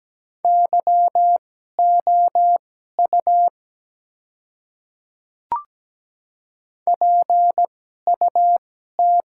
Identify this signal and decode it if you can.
Morse code, heard as a keyed tone, about 17 wpm, spaced as for 12 wpm, 700 Hz. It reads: YOU PUT
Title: 2x Character Spacing - Top 200 Words - 17wpm